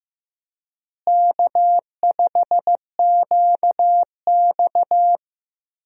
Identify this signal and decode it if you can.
Morse code, heard as a keyed tone, about 15 wpm, 700 Hz. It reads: K5QX
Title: USA Call Signs - 4 Characters - Repeated 3x - 15wpm